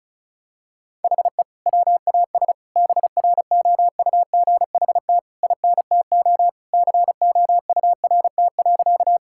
Morse code, heard as a keyed tone, about 36 wpm, 700 Hz. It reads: HE WAS BROUGHT INTO COURT.
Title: Sentences from Top 1600 Words 35wpm